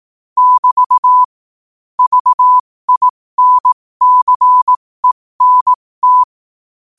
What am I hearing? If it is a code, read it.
Morse code, heard as a keyed tone, about 18 wpm, spaced as for 15 wpm, 1000 Hz. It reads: <BT> VINCENT
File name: qotdfiles / 15wpm